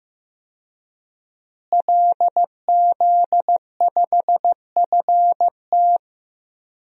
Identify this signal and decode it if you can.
Morse code, heard as a keyed tone, about 15 wpm, 700 Hz. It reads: LZ5FT
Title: DXCC-100 Call Signs - 5 Characters - Repeated 2x - 15wpm